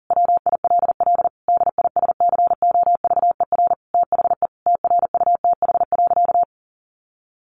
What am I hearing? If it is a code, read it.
Morse code, heard as a keyed tone, about 37 wpm, 700 Hz. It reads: WILL DISCOVER THE TRUTH.
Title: Sentences from Top 1100 Words 40wpm